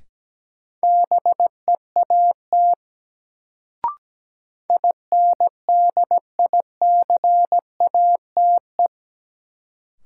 Morse code, heard as a keyed tone, about 17 wpm, 700 Hz. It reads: BEAT INDICATE
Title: Sentences from Top 1300 Words - New Word Drill Alt - 17wpm